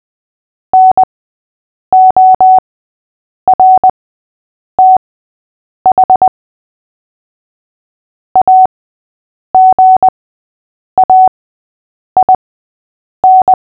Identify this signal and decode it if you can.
Morse code, heard as a keyed tone, about 20 wpm, spaced as for 8 wpm, 750 Hz. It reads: NORTH AGAIN
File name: cw / fists / Short-Story-5-8wpm